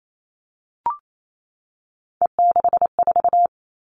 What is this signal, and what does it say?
Morse code, heard as a keyed tone, about 28 wpm, 700 Hz. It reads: E64
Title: All Cut Numbers - 3-Digit 28wpm